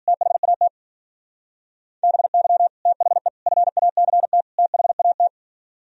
Morse code, heard as a keyed tone, about 51 wpm, 700 Hz. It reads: THAT BY THE FACT THAT